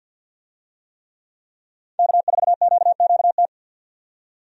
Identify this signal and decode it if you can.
Morse code, heard as a keyed tone, about 51 wpm, 700 Hz. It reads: K4QQT
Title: USA Call Signs - 5 Characters - 50wpm